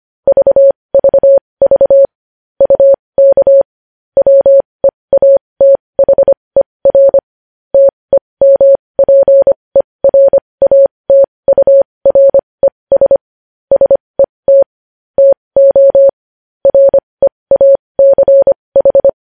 Morse code, heard as a keyed tone, about 25 wpm, spaced as for 20 wpm, 550 Hz. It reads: VVV UK WEATHER TEMPERATURES SET TO REACH
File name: podcast / News headlines-1x-25x20